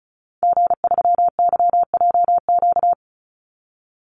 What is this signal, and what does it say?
Morse code, heard as a keyed tone, about 34 wpm, 700 Hz. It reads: G3YJQ
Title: Call Signs - 35wpm